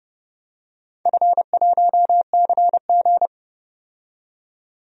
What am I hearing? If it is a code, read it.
Morse code, heard as a keyed tone, about 30 wpm, 700 Hz. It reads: F1CG
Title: DXCC-100 Call Signs - 4 Characters - 30wpm